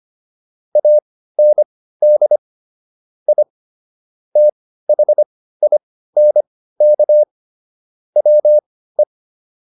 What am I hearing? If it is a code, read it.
Morse code, heard as a keyed tone, about 25 wpm, spaced as for 15 wpm, 600 Hz. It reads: AND I THINK WE